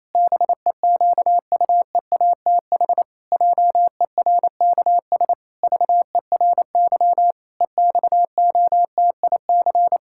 Morse code, heard as a keyed tone, about 28 wpm, 700 Hz. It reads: BEQUEATH JERKS VERY EXOTIC
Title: Pangram Challenge v2 - 28wpm